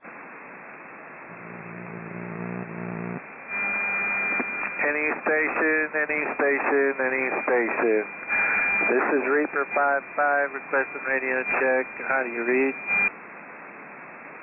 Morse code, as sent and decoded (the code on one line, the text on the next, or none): none